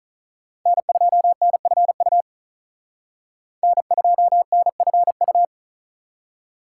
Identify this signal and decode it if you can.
Morse code, heard as a keyed tone, about 38 wpm, 700 Hz. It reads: N2NFU N2NFU
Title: Speed Racing - USA Call Signs 35wpm